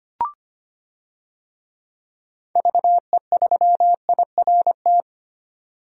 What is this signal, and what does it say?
Morse code, heard as a keyed tone, about 25 wpm, 700 Hz. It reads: VE3IRT
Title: DXCC-100 Call Signs - 6 Characters - 25wpm